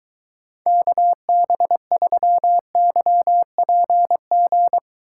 Morse code, heard as a keyed tone, about 23 wpm, 700 Hz. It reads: KB3YPG